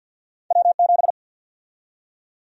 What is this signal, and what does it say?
Morse code, heard as a keyed tone, about 53 wpm, 700 Hz. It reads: W7
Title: USA Call Sign Prefixes - 50wpm